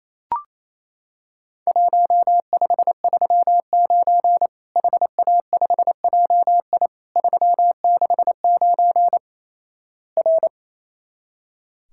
Morse code, heard as a keyed tone, about 28 wpm, 700 Hz. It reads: 1539 HA5JI 369 R
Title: Worked All Europe DX Contest - QTC Exchange - 28wpm